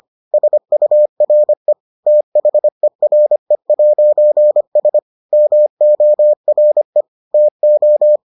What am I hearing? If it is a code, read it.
Morse code, heard as a keyed tone, about 25 wpm, 600 Hz. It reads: SURE THERE'S MORE TO